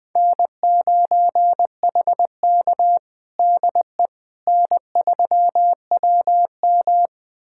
Morse code, heard as a keyed tone, about 20 wpm, 700 Hz. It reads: N9HK DE N3WM
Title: USA Callsign DE Callsign - 4 Characters - 20wpm